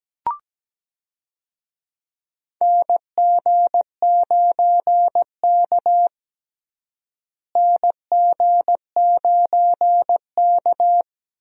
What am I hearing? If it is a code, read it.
Morse code, heard as a keyed tone, about 17 wpm, 700 Hz. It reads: NG9K NG9K
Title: USA Call Signs - 4 Characters - Repeated 2x - 17wpm